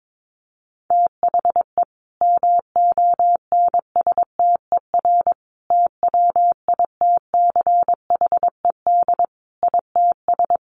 Morse code, heard as a keyed tone, about 22 wpm, 700 Hz. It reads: THE MONSTER TWITCHED ITS
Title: A Sound Of Thunder 22wpm